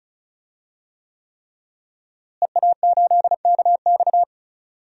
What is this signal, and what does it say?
Morse code, heard as a keyed tone, about 36 wpm, 700 Hz. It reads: EA8KX
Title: DXCC-100 Call Signs - 5 Characters - Repeated 3x - 35wpm